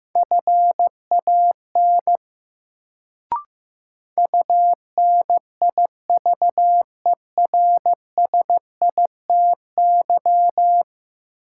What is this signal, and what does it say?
Morse code, heard as a keyed tone, about 15 wpm, 700 Hz. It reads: FAN UNIVERSITY
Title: Sentences from Top 1300 Words - New Word Drill Alt - 15wpm